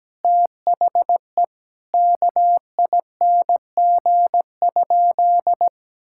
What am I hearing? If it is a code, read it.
Morse code, heard as a keyed tone, about 17 wpm, 700 Hz. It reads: THE KING?
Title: Sentences from Top 1000 Words - Review Part 6 - 17wpm